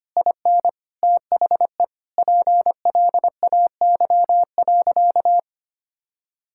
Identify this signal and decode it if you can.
Morse code, heard as a keyed tone, about 25 wpm, 700 Hz. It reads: IN THE PLAY.
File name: practice / 25wpm / Sentences with 4-Word Phrases - 25wpm